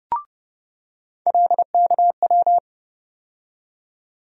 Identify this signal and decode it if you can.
Morse code, heard as a keyed tone, about 30 wpm, 700 Hz. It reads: LKW
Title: USA Call Sign Postfixes - 3 Letters 30wpm